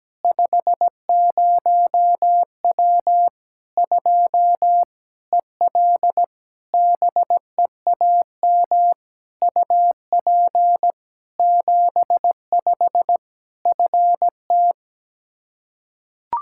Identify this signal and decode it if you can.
Morse code, heard as a keyed tone, about 17 wpm, 700 Hz. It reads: 50W 2 EL BEAM UP 75 FT
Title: Practice QSO - Exchange 2 - Info - 17wpm